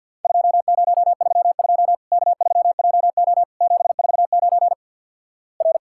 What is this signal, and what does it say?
Morse code, heard as a keyed tone, about 50 wpm, 700 Hz. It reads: 2032 K3JO 749 R
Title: Worked All Europe DX Contest - QTC Exchange - 50wpm